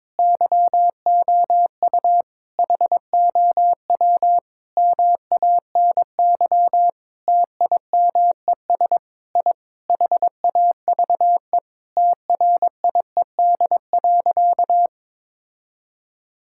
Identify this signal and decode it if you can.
Morse code, heard as a keyed tone, about 22 wpm, 700 Hz. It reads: YOU HOW MANY TIMES I HAVE TRIED.
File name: Sentences with 7-Word Phrases - 22wpm